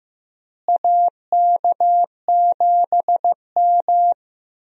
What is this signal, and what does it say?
Morse code, heard as a keyed tone, about 15 wpm, 700 Hz. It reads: AK7M
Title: USA Call Signs - 4 Characters - Repeated 2x - 15wpm